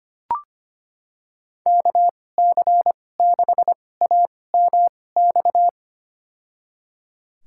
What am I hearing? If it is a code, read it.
Morse code, heard as a keyed tone, about 25 wpm, spaced as for 18 wpm, 700 Hz. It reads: KC6AMX
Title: USA Call Signs - 2x Character Spacing - 25wpm